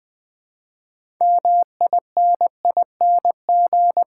Morse code, heard as a keyed tone, about 21 wpm, 700 Hz. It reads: MINING
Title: Build A Word - 3 Letters At A Time - 6-Letter Words - 20wpm